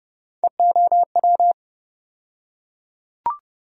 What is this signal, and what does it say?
Morse code, heard as a keyed tone, about 30 wpm, 700 Hz. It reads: EOW
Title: USA Call Sign Postfixes - 3 Letters 30wpm